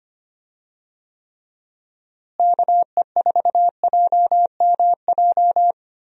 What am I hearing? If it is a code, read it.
Morse code, heard as a keyed tone, about 25 wpm, 700 Hz. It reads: KE4JMJ
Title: USA Call Signs - 6 Characters - Repeated 2x - 25wpm